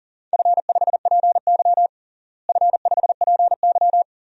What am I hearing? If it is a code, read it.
Morse code, heard as a keyed tone, about 40 wpm, 700 Hz. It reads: F5PY F5PY
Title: DXCC-100 Call Signs - 4 Characters - Repeated 2x - 40wpm